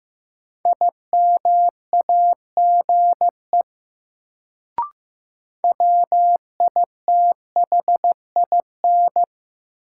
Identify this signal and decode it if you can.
Morse code, heard as a keyed tone, about 15 wpm, 700 Hz. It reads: IMAGE WITHIN
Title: Sentences from Top 700 Words - New Word Drill - 15wpm